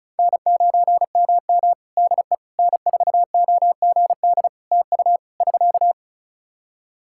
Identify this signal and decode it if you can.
Morse code, heard as a keyed tone, about 36 wpm, 700 Hz. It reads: N9MM DE N4OGD TU <SK>
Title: K1USN SST QSO Practice 35wpm